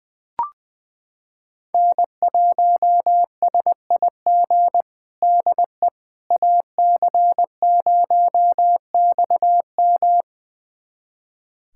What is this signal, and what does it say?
Morse code, heard as a keyed tone, about 20 wpm, 700 Hz. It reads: N1SIG DE AC0XM
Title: USA Callsign DE Callsign - 5 Characters - 20wpm